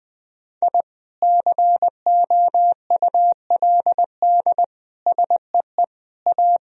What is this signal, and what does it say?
Morse code, heard as a keyed tone, about 21 wpm, 700 Hz. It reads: I COULD SEE A
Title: Sentences from Top 1000 Words - Review Part 2 - 20wpm